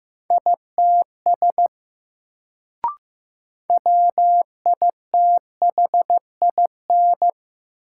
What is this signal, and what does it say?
Morse code, heard as a keyed tone, about 15 wpm, 700 Hz. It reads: ITS WITHIN
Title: Sets of 2 Words - New Word Drill - 15wpm